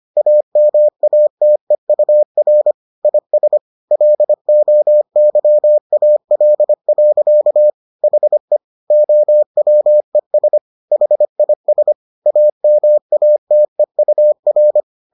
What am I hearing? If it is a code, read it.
Morse code, heard as a keyed tone, about 25 wpm, 600 Hz. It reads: AMATEUR IS LOYAL. HE OWES HIS AMATEUR